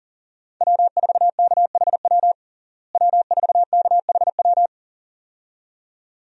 Morse code, heard as a keyed tone, about 40 wpm, 700 Hz. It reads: W4KHW W4KHW
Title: USA Call Signs - 5 Characters - Repeated 2x - 40wpm